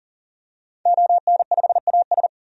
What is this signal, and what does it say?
Morse code, heard as a keyed tone, about 40 wpm, 700 Hz. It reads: ON5AS
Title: DXCC-100 Call Signs - 5 Characters - 40wpm